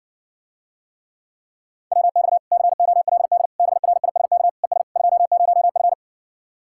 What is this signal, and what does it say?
Morse code, heard as a keyed tone, about 60 wpm, 700 Hz. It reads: WX COLD BRISK ES 20F